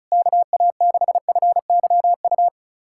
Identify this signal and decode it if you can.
Morse code, heard as a keyed tone, about 36 wpm, 700 Hz. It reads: KA6FYU
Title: USA Callsign DE Callsign - 6 Characters - 35wpm